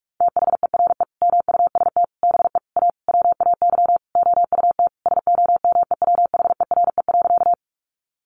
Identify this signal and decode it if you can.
Morse code, heard as a keyed tone, about 45 wpm, 700 Hz. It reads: THERE MUST BE A WAY OUT SOMEWHERE.
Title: Sentences from Top 1400 Words 45wpm